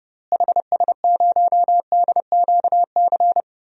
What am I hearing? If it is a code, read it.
Morse code, heard as a keyed tone, about 30 wpm, 700 Hz. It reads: HS0DQC
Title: DXCC-100 Call Signs - 6 Characters - Repeated 2x - 30wpm